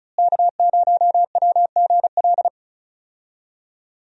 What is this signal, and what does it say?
Morse code, heard as a keyed tone, about 36 wpm, 700 Hz. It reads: K0WGL
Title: USA Call Signs - 5 Characters - Repeated 3x - 35wpm